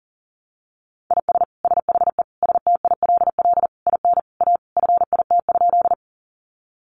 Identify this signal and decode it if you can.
Morse code, heard as a keyed tone, about 40 wpm, 700 Hz. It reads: IS SHE STILL IN A FIT?
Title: Sentences from Top 800 Words 40wpm